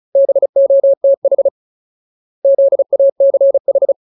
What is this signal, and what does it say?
Morse code, heard as a keyed tone, about 34 wpm, 550 Hz. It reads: BOTH ZACH
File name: Ultimate Word List / 35WPM Ultimate